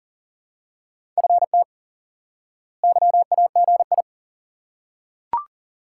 Morse code, heard as a keyed tone, about 40 wpm, 700 Hz. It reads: FT YAGI
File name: IQE Course 3 - Lesson 89 - YAGI - Two Elements - 40wpm